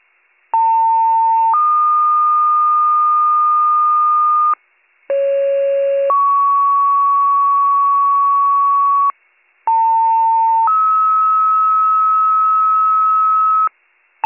Two-tone page, heard as a one-second tone, then a three-second tone, about 900 then 1220 Hz, about 550 then 1120 Hz, about 900 then 1290 Hz. It is heard from 0.5 to 13.7 s.